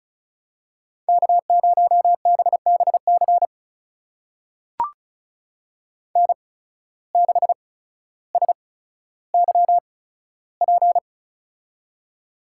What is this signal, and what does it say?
Morse code, heard as a keyed tone, about 36 wpm, spaced as for 15 wpm, 700 Hz. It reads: K0BBC N6SYP
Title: USA Call Signs - 5 Characters - 8x Character Spacing - 35wpm